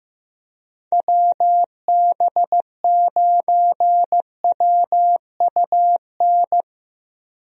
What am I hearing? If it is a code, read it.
Morse code, heard as a keyed tone, about 15 wpm, 700 Hz. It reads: WB9WUN
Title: USA Call Signs - 6 Characters - Repeated 2x - 15wpm